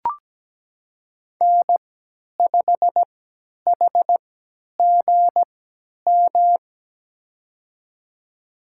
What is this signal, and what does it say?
Morse code, heard as a keyed tone, about 17 wpm, spaced as for 10 wpm, 700 Hz. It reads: N5HGM
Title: USA Call Signs - 5 Characters - 3x Character Spacing - 17wpm